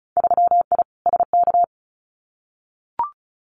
Morse code, heard as a keyed tone, about 36 wpm, 700 Hz. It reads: 3I SK
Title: Winter Field Day Exchanges Encore 35wpm